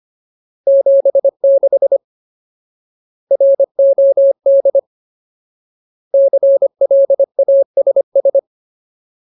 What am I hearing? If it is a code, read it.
Morse code, heard as a keyed tone, about 25 wpm, 550 Hz. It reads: 76 ROD CLASS
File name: Ultimate Word List / 25WPM Ultimate